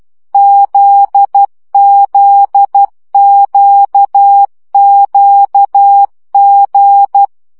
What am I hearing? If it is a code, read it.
Morse code, heard as a keyed tone, about 12 wpm, 800 Hz. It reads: ZZQQG
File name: CW gyakorlas g z q 10 baud